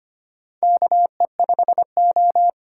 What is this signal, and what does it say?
Morse code, heard as a keyed tone, about 25 wpm, 700 Hz. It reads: KE5O
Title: USA Call Signs - 8x Character Spacing - 25wpm